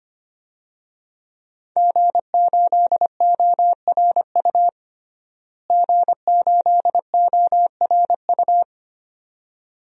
Morse code, heard as a keyed tone, about 25 wpm, 700 Hz. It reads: G8ORU G8ORU
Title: DXCC-100 Call Signs - 5 Characters - Repeated 2x - 25wpm